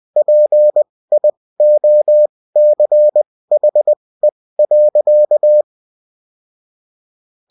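Morse code, heard as a keyed tone, about 20 wpm, spaced as for 16 wpm, 600 Hz. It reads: PIOCHE.
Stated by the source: Morse code bulletin of a French amateur CW union, sent as text and in CW.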